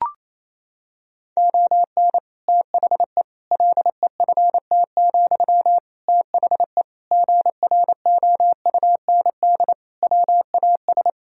Morse code, heard as a keyed tone, about 28 wpm, 700 Hz. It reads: ON THE LEFT, THE GROUND WAS
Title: Sentences from Top 1000 Words - Review Part 4 - 28wpm